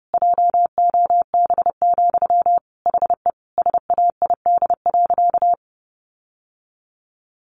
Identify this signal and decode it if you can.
Morse code, heard as a keyed tone, about 30 wpm, 700 Hz. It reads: JOB, HE SAID.
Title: Sentences from Top 1500 Words 30wpm